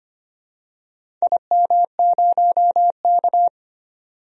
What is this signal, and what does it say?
Morse code, heard as a keyed tone, about 25 wpm, 700 Hz. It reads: IM0K
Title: DXCC-100 Call Signs - 4 Characters - 25wpm